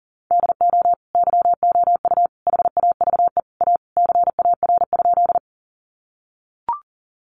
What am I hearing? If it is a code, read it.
Morse code, heard as a keyed tone, about 40 wpm, 700 Hz. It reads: DO YOU HAVE A CAR?
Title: Sentences from Top 300 Words 40wpm